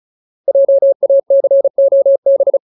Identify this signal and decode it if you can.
Morse code, heard as a keyed tone, about 35 wpm, 550 Hz. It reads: JACOB
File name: Ultimate Word List / 35WPM Ultimate